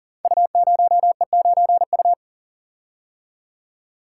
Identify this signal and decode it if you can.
Morse code, heard as a keyed tone, about 40 wpm, 700 Hz. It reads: U0E9U
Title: Code Groups - 5 Letter:Numbers - 40wpm